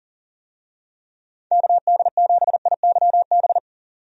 Morse code, heard as a keyed tone, about 40 wpm, 700 Hz. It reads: KD7IYB
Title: USA Call Signs - 6 Characters - 40wpm